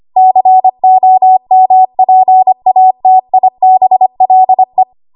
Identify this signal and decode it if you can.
Morse code, heard as a keyed tone, about 25 wpm, 750 Hz. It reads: COMPATIBLE
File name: Archive / 25 WPM / 25WPM